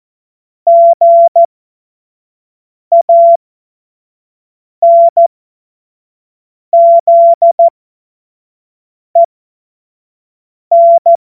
Morse code, heard as a keyed tone, about 14 wpm, spaced as for 5 wpm, 680 Hz. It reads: GANZEN
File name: Morsekurs / Witze / Witz 15. November 2022 25 F70